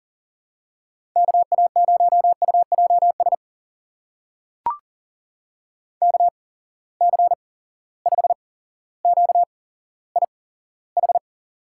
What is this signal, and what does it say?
Morse code, heard as a keyed tone, about 40 wpm, spaced as for 17 wpm, 700 Hz. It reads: KA0UJS KC5QIH
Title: USA Call Signs - 6 Characters - 8x Character Spacing - 40wpm